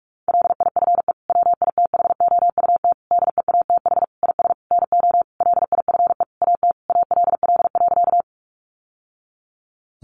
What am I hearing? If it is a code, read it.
Morse code, heard as a keyed tone, about 44 wpm, 700 Hz. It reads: LIFE WITHOUT DEATH IS NO LIFE AT ALL.
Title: Sentences from Top 1100 Words 45wpm